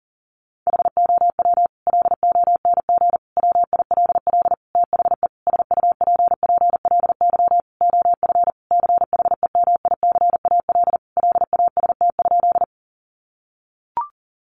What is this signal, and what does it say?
Morse code, heard as a keyed tone, about 40 wpm, 700 Hz. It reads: HOW LONG WILL THE SUPPLY OF CHEMICAL LAST?
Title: Sentences from Top 1200 Words 40wpm